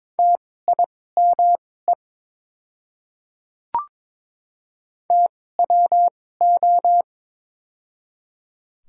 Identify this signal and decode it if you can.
Morse code, heard as a keyed tone, about 22 wpm, spaced as for 16 wpm, 700 Hz. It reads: TIME TWO
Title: 2x Character Spacing - Top 100 Words - 22wpm